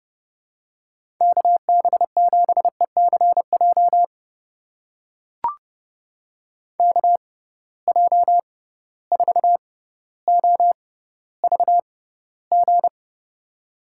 Morse code, heard as a keyed tone, about 30 wpm, spaced as for 15 wpm, 700 Hz. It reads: KB7ECJ KJ4OVG
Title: USA Call Signs - 6 Characters - 6x Character Spacing - 30wpm